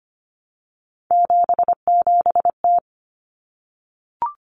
Morse code, heard as a keyed tone, about 25 wpm, 700 Hz. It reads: 77T
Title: Common Cut Numbers - 3-Digit 25wpm